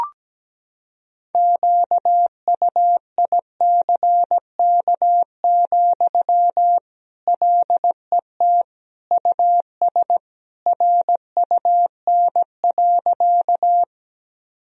Morse code, heard as a keyed tone, about 17 wpm, 700 Hz. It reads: QUICK, LET US RUN.
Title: Sentences from Top 1000 Words - Review Part 1 - 17wpm